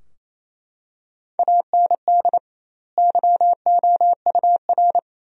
Morse code, heard as a keyed tone, about 28 wpm, 700 Hz. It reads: AND YOUR
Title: 2x Word-Spacing - Sentences from Top 300 Words - 28wpm